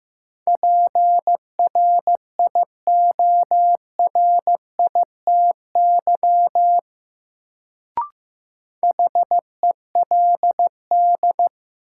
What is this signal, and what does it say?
Morse code, heard as a keyed tone, about 15 wpm, 700 Hz. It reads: PRIORITY HELD
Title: Sentences from Top 1600 Words - New Word Drill - 15wpm